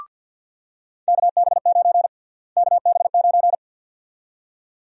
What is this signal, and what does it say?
Morse code, heard as a keyed tone, about 47 wpm, 700 Hz. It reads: KB9 KB9
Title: USA Call Sign Prefixes - Repeated 2x - 50wpm